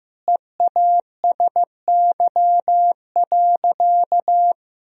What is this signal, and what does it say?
Morse code, heard as a keyed tone, about 15 wpm, 700 Hz. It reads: EASY.
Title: Sentences from Top 1000 Words - Review Part 2 - 15wpm